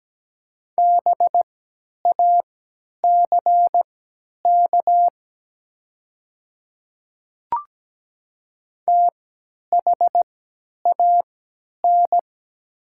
Morse code, heard as a keyed tone, about 17 wpm, spaced as for 10 wpm, 700 Hz. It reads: BACK THAN
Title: 3x Character Spacing - Top 300 Words - 17wpm